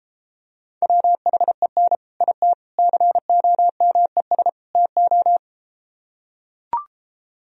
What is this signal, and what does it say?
Morse code, heard as a keyed tone, about 33 wpm, 700 Hz. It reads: WHEN IT COMES TO